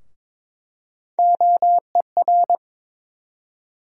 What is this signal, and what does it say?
Morse code, heard as a keyed tone, about 22 wpm, 700 Hz. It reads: OER
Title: USA Call Sign Postfixes - 3 Letters 22wpm